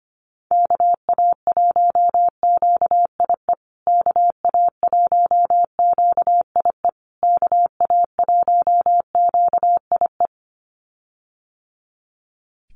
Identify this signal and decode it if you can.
Morse code, heard as a keyed tone, about 25 wpm, 700 Hz. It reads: KA1QIE KA1QIE KA1QIE
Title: USA Call Signs Repeated 3x 25wpm